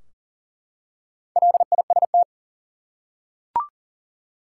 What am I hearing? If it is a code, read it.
Morse code, heard as a keyed tone, about 40 wpm, 700 Hz. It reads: LIST
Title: Four Letter Words - 40wpm